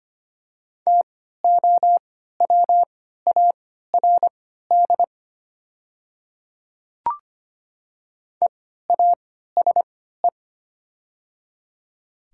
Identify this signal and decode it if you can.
Morse code, heard as a keyed tone, about 25 wpm, spaced as for 14 wpm, 700 Hz. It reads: TOWARD EASE
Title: 3x Character Spacing - Top 500 Words - 25wpm